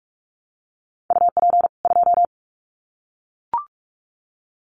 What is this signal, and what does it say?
Morse code, heard as a keyed tone, about 44 wpm, 700 Hz. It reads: UP 2
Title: Common QSO Phrases 45wpm